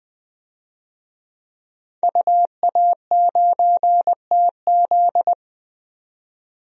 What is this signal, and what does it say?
Morse code, heard as a keyed tone, about 21 wpm, 700 Hz. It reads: UA9TZ
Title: DXCC-100 Call Signs - 5 Characters - 20wpm